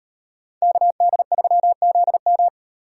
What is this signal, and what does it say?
Morse code, heard as a keyed tone, about 37 wpm, 700 Hz. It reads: KD3ZM